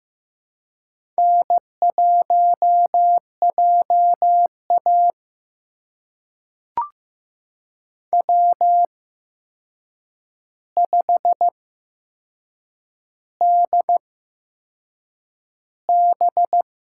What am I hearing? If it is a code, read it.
Morse code, heard as a keyed tone, about 15 wpm, spaced as for 6 wpm, 700 Hz. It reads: N1JA W5DB
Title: USA Call Signs - 4 Characters - 8x Character Spacing - 15wpm